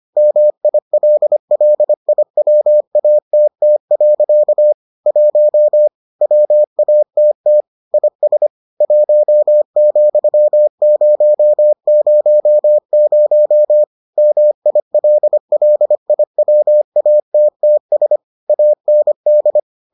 Morse code, heard as a keyed tone, about 25 wpm, 600 Hz. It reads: MILLIWATT. 1 WATT IS 1,000 MILLIWATTS AND